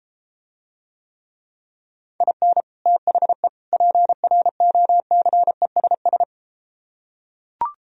Morse code, heard as a keyed tone, about 33 wpm, 700 Hz. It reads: IN THE PROCESS